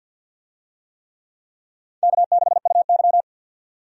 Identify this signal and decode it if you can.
Morse code, heard as a keyed tone, about 49 wpm, 700 Hz. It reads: K6UY
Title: USA Call Signs - 4 Characters - 50wpm